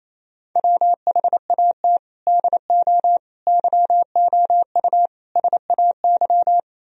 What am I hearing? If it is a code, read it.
Morse code, heard as a keyed tone, about 35 wpm, 700 Hz. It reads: WHAT DO YOU SAY
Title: Speed-Racing - Sets of 5 Words 28wpm